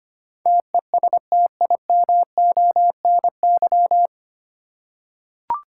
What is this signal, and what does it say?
Morse code, heard as a keyed tone, about 25 wpm, 700 Hz. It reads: TESTIMONY
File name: Nine Letter Words - 25wpm